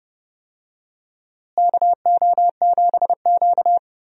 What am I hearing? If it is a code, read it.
Morse code, heard as a keyed tone, about 31 wpm, 700 Hz. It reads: KO7Q
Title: USA Call Signs - 4 Characters - Repeated 2x - 30wpm